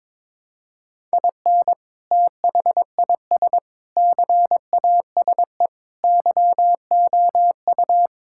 Morse code, heard as a keyed tone, about 23 wpm, 700 Hz. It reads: IN THIS CASE YOU
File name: practice / 22wpm / Sentences with 3-Word Phrases - 22wpm